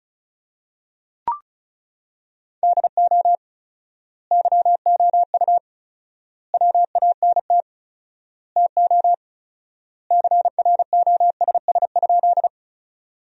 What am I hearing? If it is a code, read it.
Morse code, heard as a keyed tone, about 34 wpm, 700 Hz. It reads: DO YOU WANT TO CROSS?
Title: 4x Word-Spacing - Sentences from Top 300 Words - 35wpm